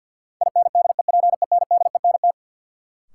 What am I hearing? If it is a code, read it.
Morse code, heard as a keyed tone, about 52 wpm, 700 Hz. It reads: INDEPENDENT
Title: Sentences from Top 900 Words - New Word Drill - 50wpm